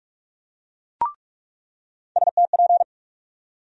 Morse code, heard as a keyed tone, about 43 wpm, 700 Hz. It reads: STP
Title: USA Call Sign Postfixes - 3 Letters 45wpm